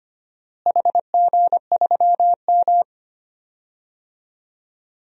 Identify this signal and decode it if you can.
Morse code, heard as a keyed tone, about 25 wpm, 700 Hz. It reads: HG3M
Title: DXCC-100 Call Signs - 4 Characters - Repeated 3x - 25wpm